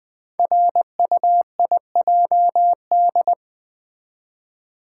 Morse code, heard as a keyed tone, about 20 wpm, 700 Hz. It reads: RUIJD
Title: Code Groups - 5 Letters - 20wpm